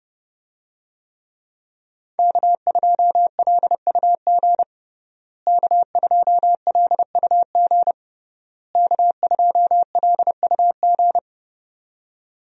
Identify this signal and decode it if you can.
Morse code, heard as a keyed tone, about 30 wpm, 700 Hz. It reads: K2LUG K2LUG K2LUG
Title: USA Call Signs - 5 Characters - Repeated 3x - 30wpm